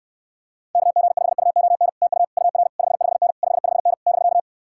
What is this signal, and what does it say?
Morse code, heard as a keyed tone, about 67 wpm, 700 Hz. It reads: KG5UON TU UR 55N 55N <BK>